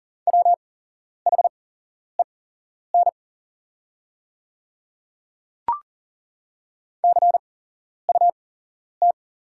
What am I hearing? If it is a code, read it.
Morse code, heard as a keyed tone, about 40 wpm, spaced as for 11 wpm, 690 Hz. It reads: WHEN CUT
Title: 8x Character Spacing - Top 300 Words - 40wpm